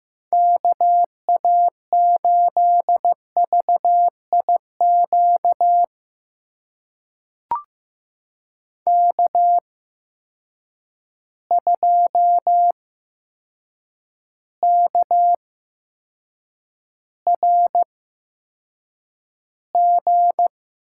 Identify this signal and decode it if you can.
Morse code, heard as a keyed tone, about 15 wpm, spaced as for 7 wpm, 700 Hz. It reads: KA8VIQ K2KRG
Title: USA Call Signs - 8x Character Spacing - 15wpm